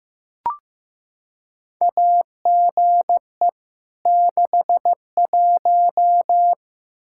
Morse code, heard as a keyed tone, about 15 wpm, 700 Hz. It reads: AGE 61
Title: Practice QSO - Exchange 3 - More Info - 15wpm